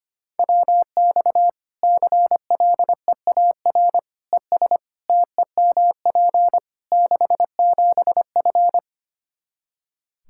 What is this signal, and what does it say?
Morse code, heard as a keyed tone, about 25 wpm, 700 Hz. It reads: WX CLEAR ES TEMP 67F
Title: K1USN SST QSO Practice 25wpm